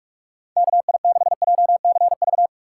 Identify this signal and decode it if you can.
Morse code, heard as a keyed tone, about 58 wpm, 700 Hz. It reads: KI6JCV